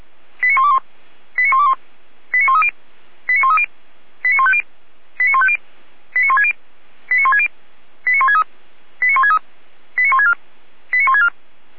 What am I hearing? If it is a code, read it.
Five-tone selective call (ZVEI), heard as quick runs of five tones, tones of about 80 ms. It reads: 89121 89121 89130 89130 89150 89150 89160 89160 89163 89163 89164 89164
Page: Server at audio.f85360.feuerwehr-orlamuende.org